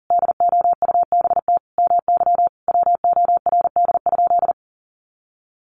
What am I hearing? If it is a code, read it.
Morse code, heard as a keyed tone, about 40 wpm, 700 Hz. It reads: DOUBT MY WORD?
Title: Sentences from Top 1500 Words 40wpm